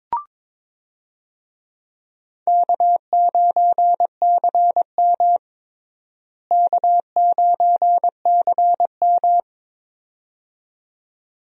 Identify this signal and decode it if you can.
Morse code, heard as a keyed tone, about 22 wpm, 700 Hz. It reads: K9CM K9CM
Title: USA Call Signs - 4 Characters - Repeated 2x - 22wpm